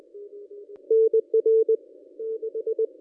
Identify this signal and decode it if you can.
Morse code, heard as a keyed tone, about 20 wpm, 440 Hz. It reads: NR 6
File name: contests / ss-cw-k5tr-op-n5rz -2200